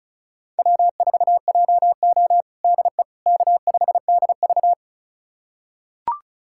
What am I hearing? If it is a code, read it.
Morse code, heard as a keyed tone, about 34 wpm, 700 Hz. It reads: W4JO DE K5DV
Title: USA Callsign DE Callsign - 4 Characters - 35wpm